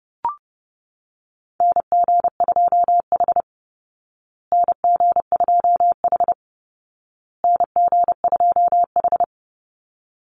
Morse code, heard as a keyed tone, about 30 wpm, 700 Hz. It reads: NG2H NG2H NG2H
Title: Call Signs Repeated 2x - 30wpm